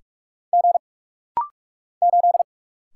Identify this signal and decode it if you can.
Morse code, heard as a keyed tone, about 46 wpm, 700 Hz. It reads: G 8
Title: Single Letter-Number - Rapid-Fire 45wpm